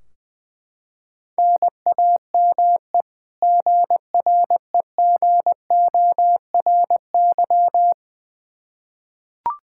Morse code, heard as a keyed tone, about 20 wpm, 700 Hz. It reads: NAME GREGORY
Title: Practice QSO - Exchange 1 - Beginning - 20wpm